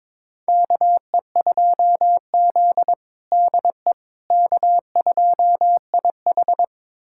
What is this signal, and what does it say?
Morse code, heard as a keyed tone, about 21 wpm, 700 Hz. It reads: KE2Z DE K2IH
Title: USA Callsign DE Callsign - 4 Characters - 22wpm